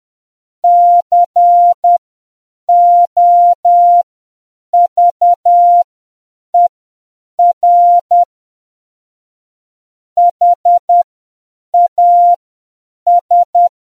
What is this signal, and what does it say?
Morse code, heard as a keyed tone, about 10 wpm, spaced as for 7 wpm, 700 Hz. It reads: COVER HAS